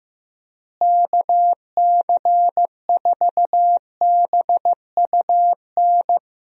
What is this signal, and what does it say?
Morse code, heard as a keyed tone, about 14 wpm, 700 Hz. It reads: KC4BUN
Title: USA Call Signs - 6 Characters - Repeated 2x - 15wpm